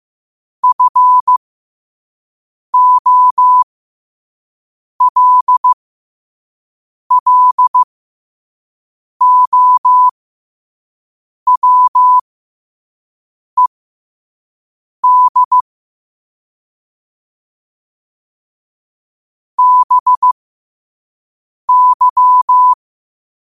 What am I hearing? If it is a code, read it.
Morse code, heard as a keyed tone, about 15 wpm, spaced as for 5 wpm, 1000 Hz. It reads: FOLLOWED BY
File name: qotdfiles / 5wpm